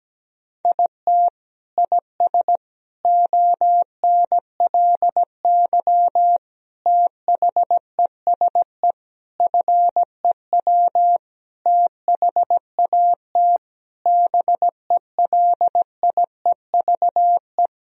Morse code, heard as a keyed tone, about 17 wpm, 700 Hz. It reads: IT IS ONLY THESE FEW THAT BELIEVE
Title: Sentences from Top 1000 Words - Review Part 1 - 17wpm